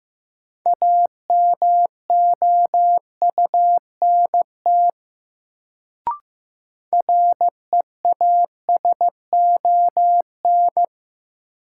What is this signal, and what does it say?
Morse code, heard as a keyed tone, about 15 wpm, 700 Hz. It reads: AMOUNT REASON
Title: Sentences from Top 700 Words - New Word Drill - 15wpm